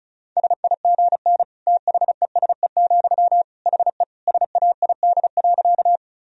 Morse code, heard as a keyed tone, about 36 wpm, 700 Hz. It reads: SIGN THESE, HE SAID.
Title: Sentences from Top 1000 Words - Review Part 1 - 35wpm